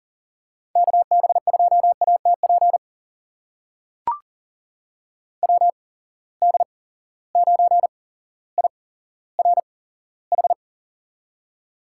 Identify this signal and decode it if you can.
Morse code, heard as a keyed tone, about 38 wpm, spaced as for 17 wpm, 700 Hz. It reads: KB2ATP WD9IRH
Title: USA Call Signs - 6 Characters - 8x Character Spacing - 40wpm